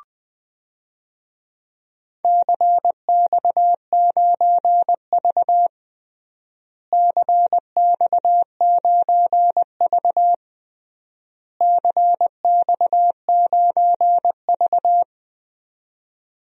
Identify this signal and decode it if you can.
Morse code, heard as a keyed tone, about 21 wpm, 700 Hz. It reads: CX9V CX9V CX9V
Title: DXCC-100 Call Signs - 4 Characters - Repeated 3x - 20wpm